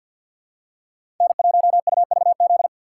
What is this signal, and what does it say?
Morse code, heard as a keyed tone, about 53 wpm, 700 Hz. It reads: N1UVZ
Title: USA Call Signs - 5 Characters - 8x Character Spacing - 50wpm